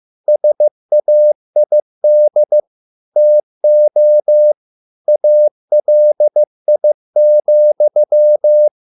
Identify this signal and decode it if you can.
Morse code, heard as a keyed tone, about 15 wpm, 600 Hz. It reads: SAID TO ALI,